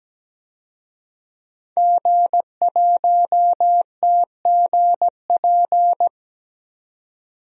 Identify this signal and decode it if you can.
Morse code, heard as a keyed tone, about 17 wpm, 700 Hz. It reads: G1TGP
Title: DXCC-100 Call Signs - 5 Characters - 17wpm